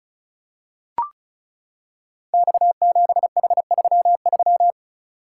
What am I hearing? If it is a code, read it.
Morse code, heard as a keyed tone, about 34 wpm, 700 Hz. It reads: X7H33
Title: Code Groups - 5 Letter:Numbers - 35wpm